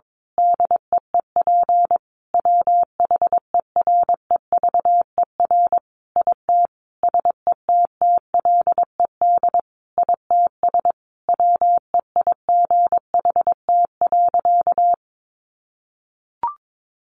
Morse code, heard as a keyed tone, about 22 wpm, 700 Hz. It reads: DEEP WHEREVER IT SETTLED ITS WEIGHT.
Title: A Sound Of Thunder 22wpm